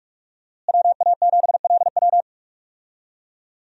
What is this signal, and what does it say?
Morse code, heard as a keyed tone, about 43 wpm, 700 Hz. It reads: WA7LW
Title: USA Call Signs - 5 Characters - Repeated 3x - 45wpm